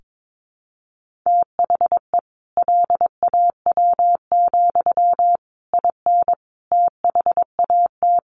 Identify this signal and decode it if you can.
Morse code, heard as a keyed tone, about 22 wpm, 700 Hz. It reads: THE LAW, IN THAT
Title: Sentences from Top 1400 Words 22wpm